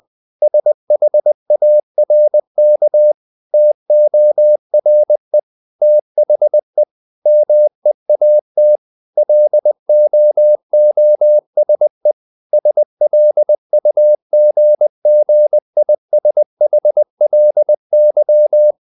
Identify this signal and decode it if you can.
Morse code, heard as a keyed tone, about 20 wpm, 600 Hz. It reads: SHARK TORE THE MEAT LOOSE SLUGGISHLY
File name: the old man and the sea part 2 20wpm